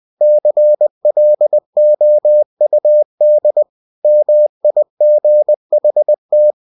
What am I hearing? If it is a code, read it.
Morse code, heard as a keyed tone, about 20 wpm, 600 Hz. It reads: CLOUD MIGHT